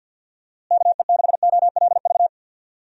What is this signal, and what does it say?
Morse code, heard as a keyed tone, about 46 wpm, 700 Hz. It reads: KE6OLV
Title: USA Call Signs - 6 Characters - Repeated 3x - 50wpm